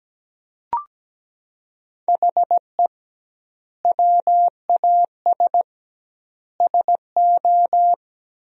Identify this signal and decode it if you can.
Morse code, heard as a keyed tone, about 17 wpm, 700 Hz. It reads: HE WAS SO
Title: 2x Word-Spacing - Sentences from Top 200 Words - 17wpm